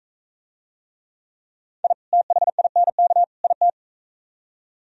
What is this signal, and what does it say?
Morse code, heard as a keyed tone, about 45 wpm, 700 Hz. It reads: I THINK IT